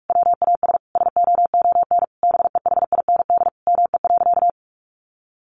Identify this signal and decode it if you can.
Morse code, heard as a keyed tone, about 44 wpm, 700 Hz. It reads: WAS SOON BEHIND ME.
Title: Sentences from Top 500 Words 45wpm